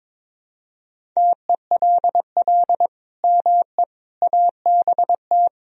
Morse code, heard as a keyed tone, about 22 wpm, 700 Hz. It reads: TELL ME ABT
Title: Sentences from Top 200 Words and Abbreviations 22wpm